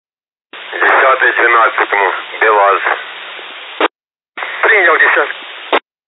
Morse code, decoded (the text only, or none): none